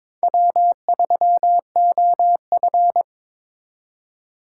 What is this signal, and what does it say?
Morse code, heard as a keyed tone, about 26 wpm, 700 Hz. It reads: W3OF
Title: Speed Racing - USA 1x2 Call Signs 22wpm